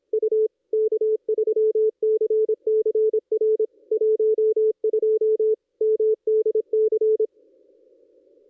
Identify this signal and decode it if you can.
Morse code, heard as a keyed tone, about 26 wpm, 440 Hz. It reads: U K3CCR 12 MDC
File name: contests / ss-cw-k5tr-op-n5rz -1400